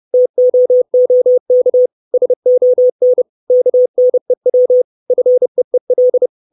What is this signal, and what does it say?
Morse code, heard as a keyed tone, about 30 wpm, 500 Hz. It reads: TOOK SON KNEW FEEL
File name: 632 Words 30WPM